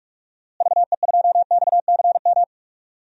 Morse code, heard as a keyed tone, about 46 wpm, 700 Hz. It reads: VE2XCM
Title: DXCC-100 Call Signs - 6 Characters - Repeated 2x - 45wpm